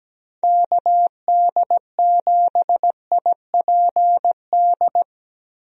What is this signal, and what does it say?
Morse code, heard as a keyed tone, about 17 wpm, 700 Hz. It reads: KD7IPD
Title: USA Call Signs - 6 Characters - Repeated 2x - 17wpm